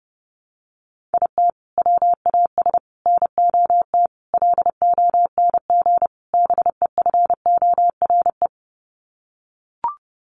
Morse code, heard as a keyed tone, about 30 wpm, 700 Hz. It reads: IT WAS NOT LONG BEFORE
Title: Sets of 5 Words Encore2 30wpm